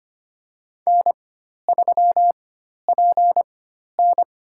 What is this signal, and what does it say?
Morse code, heard as a keyed tone, about 25 wpm, spaced as for 12 wpm, 700 Hz. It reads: N3PN
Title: USA Call Signs - 4x Character Spacing - 25wpm